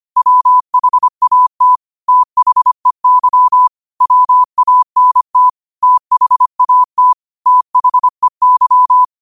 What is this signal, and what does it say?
Morse code, heard as a keyed tone, about 25 wpm, 1000 Hz. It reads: WHAT THEY WANT THAT THEY
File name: qotdfiles / 25wpm